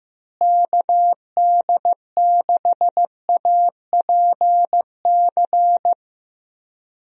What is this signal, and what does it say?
Morse code, heard as a keyed tone, about 15 wpm, 700 Hz. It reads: KD6APC
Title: USA Call Signs - 6 Characters - Repeated 2x - 15wpm